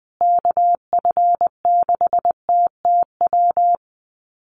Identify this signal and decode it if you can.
Morse code, heard as a keyed tone, about 20 wpm, 700 Hz. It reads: KF6TTW
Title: USA Call Signs Repeated 3x 20wpm